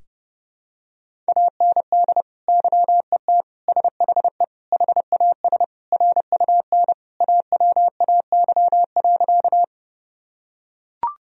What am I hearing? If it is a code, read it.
Morse code, heard as a keyed tone, about 32 wpm, 700 Hz. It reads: AND YET SHE HAS RUN AWAY.
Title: Sentences from Top 1000 Words - Review Part 1 - 30wpm